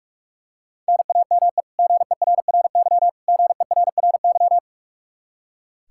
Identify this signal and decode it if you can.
Morse code, heard as a keyed tone, about 45 wpm, 700 Hz. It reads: NAME GERRY GERRY
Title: Practice QSO - Exchange 1 - Beginning - 45wpm